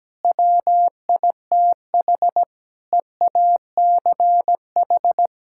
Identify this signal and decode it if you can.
Morse code, heard as a keyed tone, about 17 wpm, 700 Hz. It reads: WITH EACH
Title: Sentences from Top 1000 Words - Review Part 6 - 17wpm